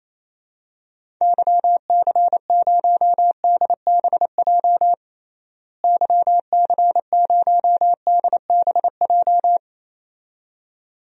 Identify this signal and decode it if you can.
Morse code, heard as a keyed tone, about 28 wpm, 700 Hz. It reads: YC0DBJ YC0DBJ
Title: DXCC-100 Call Signs - 6 Characters - Repeated 2x - 28wpm